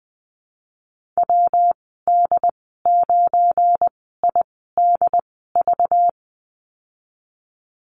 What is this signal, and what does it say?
Morse code, heard as a keyed tone, about 20 wpm, 700 Hz. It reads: W D 9 I D V
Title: 238 Lesson 38g - Letter q - Call Signs 20wpm